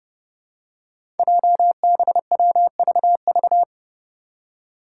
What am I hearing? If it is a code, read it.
Morse code, heard as a keyed tone, about 28 wpm, 700 Hz. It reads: JBWVV
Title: Five Challenging Letter-Number Combinations 30wpm